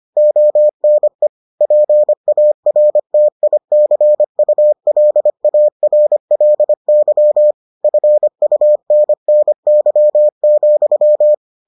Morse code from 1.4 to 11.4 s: PARTICULARLY FUNNY,

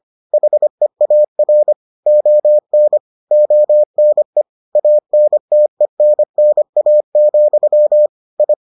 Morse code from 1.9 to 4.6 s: ON ONE

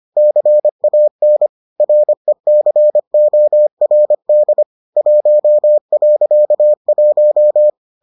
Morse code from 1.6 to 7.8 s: RECORD 1.1

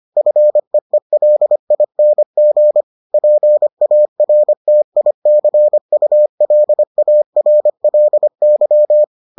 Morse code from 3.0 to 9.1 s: PARTICULARLY

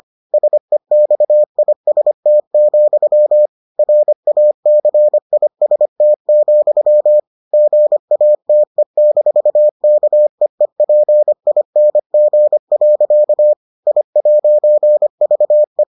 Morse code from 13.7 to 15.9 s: I'VE

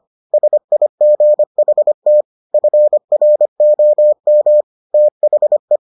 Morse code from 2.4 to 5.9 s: FROM THE

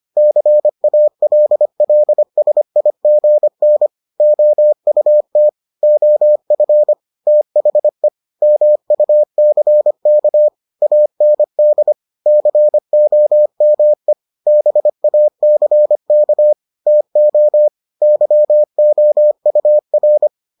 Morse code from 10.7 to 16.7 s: AND COME BACK

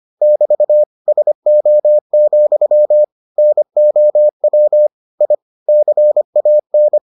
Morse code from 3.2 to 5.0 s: NOW

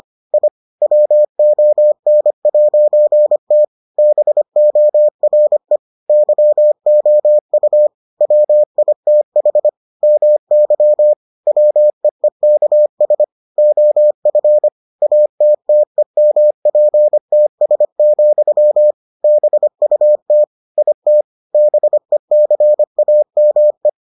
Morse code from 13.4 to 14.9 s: OF

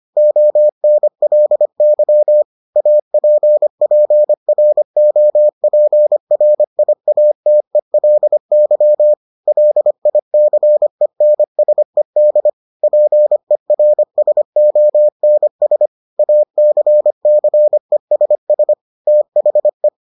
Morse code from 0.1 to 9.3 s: ONLY APPROPRIATELY

